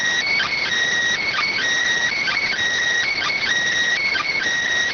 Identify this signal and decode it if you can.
Emergency Alert System (SAME) tones, heard as harsh buzzing bursts. It is heard from 0.3 to 4.4 s.